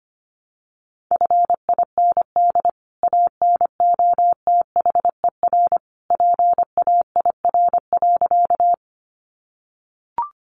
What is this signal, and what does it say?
Morse code, heard as a keyed tone, about 25 wpm, 700 Hz. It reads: FIND ANOTHER PAIR.
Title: Sentences from Top 1200 Words 25wpm